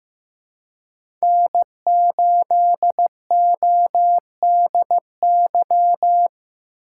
Morse code, heard as a keyed tone, about 15 wpm, 700 Hz. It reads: N8ODY
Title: USA Call Signs - 5 Characters - 3x Character Spacing - 15wpm